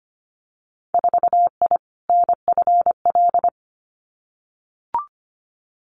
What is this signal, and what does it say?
Morse code, heard as a keyed tone, about 25 wpm, 700 Hz. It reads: 4I NFL
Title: Winter Field Day Exchanges 25wpm